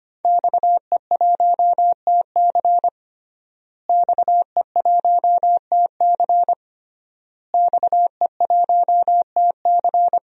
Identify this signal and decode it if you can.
Morse code, heard as a keyed tone, about 25 wpm, 700 Hz. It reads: XE1TC XE1TC XE1TC
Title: DXCC-100 Call Signs - 5 Characters - Repeated 3x - 25wpm